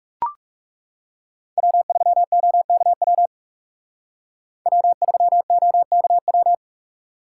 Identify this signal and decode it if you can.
Morse code, heard as a keyed tone, about 38 wpm, 700 Hz. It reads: W3OKW W3OKW